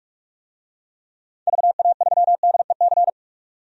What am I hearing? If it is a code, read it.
Morse code, heard as a keyed tone, about 41 wpm, 700 Hz. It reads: VA3DEC
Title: DXCC-100 Call Signs - 6 Characters - 45wpm